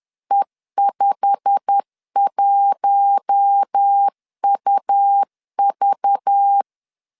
Morse code, heard as a keyed tone, about 10 wpm, 800 Hz. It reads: E51UV
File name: LEZIONI / 8 YJ45 / 8 YJ45 10wpm 2549